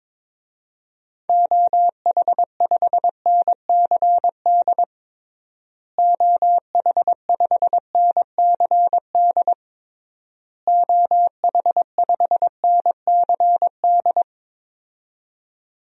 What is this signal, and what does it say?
Morse code, heard as a keyed tone, about 22 wpm, 700 Hz. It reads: OH5NCD OH5NCD OH5NCD
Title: DXCC-100 Call Signs - 6 Characters - Repeated 3x - 22wpm